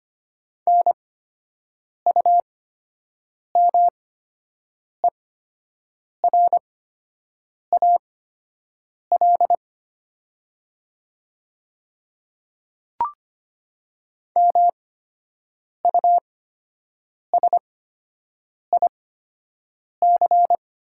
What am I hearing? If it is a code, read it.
Morse code, heard as a keyed tone, about 25 wpm, spaced as for 7 wpm, 700 Hz. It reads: NUMERAL MUSIC
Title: 8x Character Spacing - Top 500 Words - 25wpm